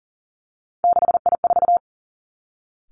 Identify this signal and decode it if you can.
Morse code, heard as a keyed tone, about 42 wpm, 700 Hz. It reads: 6I4
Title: Three Challenging Letter-Number Combinations 40wpm